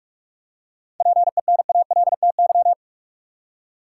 Morse code, heard as a keyed tone, about 46 wpm, 700 Hz. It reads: PENALTY
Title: Seven Letter Words - 45wpm